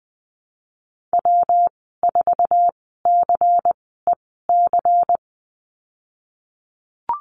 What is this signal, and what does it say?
Morse code, heard as a keyed tone, about 20 wpm, spaced as for 15 wpm, 700 Hz. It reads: W4CEC
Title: 179 Lesson 29g - Slash - Call Signs 20wpm